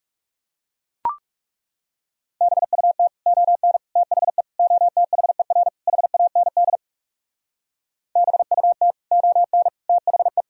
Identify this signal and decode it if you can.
Morse code, heard as a keyed tone, about 38 wpm, 700 Hz. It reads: BUT ON THE OTHER HAND BUT ON THE